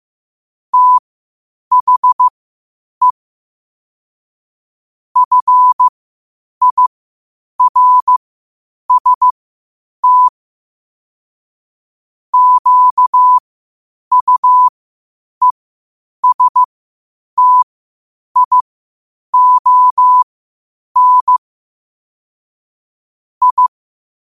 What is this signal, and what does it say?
Morse code, heard as a keyed tone, about 15 wpm, spaced as for 8 wpm, 1000 Hz. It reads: THE FIRST QUESTION I
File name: qotdfiles / 8wpm